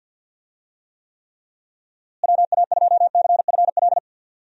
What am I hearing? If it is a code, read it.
Morse code, heard as a keyed tone, about 50 wpm, 700 Hz. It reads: WA2CFL
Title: USA Call Signs - 6 Characters - Repeated 3x - 50wpm